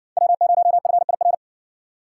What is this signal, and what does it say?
Morse code, heard as a keyed tone, about 56 wpm, 700 Hz. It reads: W0FIR